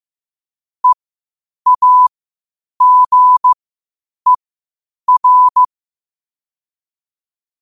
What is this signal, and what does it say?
Morse code, heard as a keyed tone, about 15 wpm, spaced as for 8 wpm, 1000 Hz. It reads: EAGER
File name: 8wpm